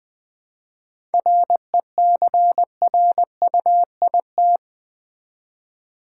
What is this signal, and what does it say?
Morse code, heard as a keyed tone, about 20 wpm, 690 Hz. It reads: RECRUIT
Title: 7 Letter Words - First 3 Letters Given - 20wpm